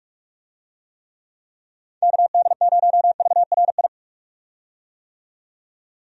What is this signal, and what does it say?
Morse code, heard as a keyed tone, about 43 wpm, 700 Hz. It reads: KD0VRI
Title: USA Call Signs - 6 Characters - 45wpm